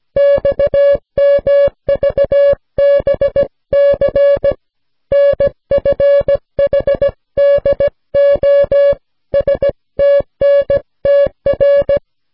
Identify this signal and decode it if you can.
Morse code, heard as a keyed tone, about 16 wpm, 560 Hz. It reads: XMVBC NFHDO STNTR